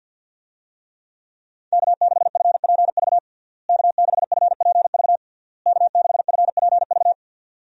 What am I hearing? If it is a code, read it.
Morse code, heard as a keyed tone, about 49 wpm, 700 Hz. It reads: K6FPV K6FPV K6FPV
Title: USA Call Signs - 5 Characters - Repeated 3x - 50wpm